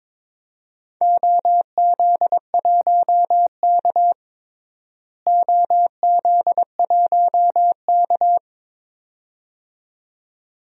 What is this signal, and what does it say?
Morse code, heard as a keyed tone, about 22 wpm, 700 Hz. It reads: OZ1K OZ1K
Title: DXCC-100 Call Signs - 4 Characters - Repeated 2x - 22wpm